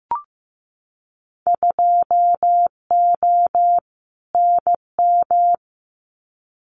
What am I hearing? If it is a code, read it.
Morse code, heard as a keyed tone, about 15 wpm, 700 Hz. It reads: 2O NM
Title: Winter Field Day Exchanges 15wpm